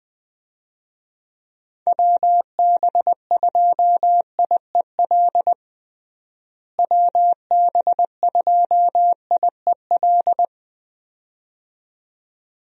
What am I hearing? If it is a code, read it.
Morse code, heard as a keyed tone, about 21 wpm, 700 Hz. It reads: WB2IEL WB2IEL
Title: USA Call Signs - 6 Characters - Repeated 2x - 20wpm